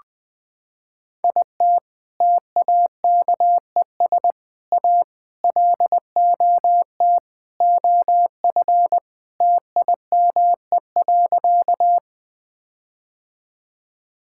Morse code, heard as a keyed tone, about 21 wpm, 700 Hz. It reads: IT TAKES A LOT OF TIME.
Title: Sentences with 4-Word Phrases - 20wpm